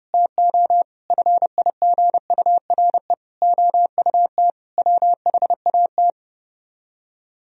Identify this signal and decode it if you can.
Morse code, heard as a keyed tone, about 33 wpm, 700 Hz. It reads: TO FIGURE OUT WHAT